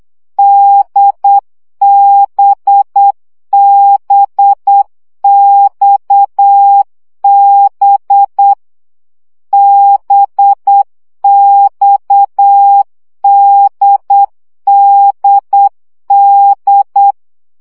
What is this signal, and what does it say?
Morse code, heard as a keyed tone, about 8 wpm, 800 Hz. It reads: DBBXB BXDDD
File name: CW gyakorlas d b x 7 baud